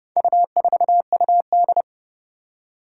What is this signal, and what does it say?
Morse code, heard as a keyed tone, about 31 wpm, 700 Hz. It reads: U4UD
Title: Four Challenging Letter-Number Combinations 30wpm